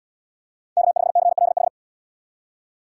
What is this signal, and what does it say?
Morse code, heard as a keyed tone, about 77 wpm, 700 Hz. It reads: K5CLH